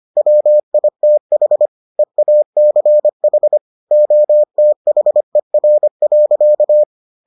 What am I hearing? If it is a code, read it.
Morse code, heard as a keyed tone, about 25 wpm, 600 Hz. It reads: WITH EACH OTHER.